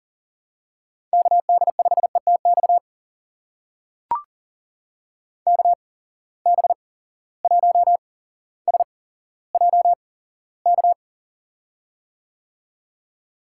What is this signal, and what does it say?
Morse code, heard as a keyed tone, about 41 wpm, spaced as for 17 wpm, 700 Hz. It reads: KD5ETX KB1SJK
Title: USA Call Signs - 6 Characters - 8x Character Spacing - 40wpm